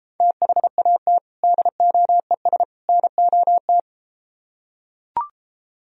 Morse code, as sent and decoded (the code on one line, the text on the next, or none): - .... .- - / -.. --- . ... / -. --- -
THAT DOES NOT